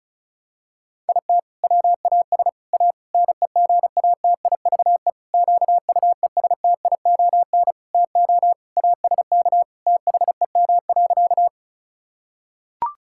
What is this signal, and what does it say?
Morse code, heard as a keyed tone, about 34 wpm, 700 Hz. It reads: IT WAS A NEGATIVE QUESTION TO ASK THEM.
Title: Sentences from Top 1000 Words - Review Part 6 - 35wpm